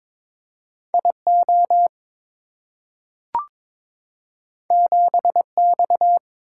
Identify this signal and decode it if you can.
Morse code, heard as a keyed tone, about 22 wpm, 700 Hz. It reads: IO 7X
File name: practice / 22wpm / Two Letter-Number Combinations - 22wpm